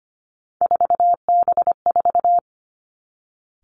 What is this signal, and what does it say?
Morse code, heard as a keyed tone, about 25 wpm, 700 Hz. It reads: 4B4
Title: Three Challenging Letter-Number Combinations 25wpm